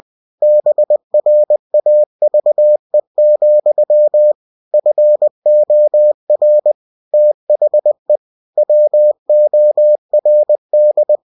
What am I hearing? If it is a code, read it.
Morse code, heard as a keyed tone, about 20 wpm, 600 Hz. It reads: BRAVE, FOR THE WORD
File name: cw-books / winnie-the-pooh / winnie-20wpm-page-0040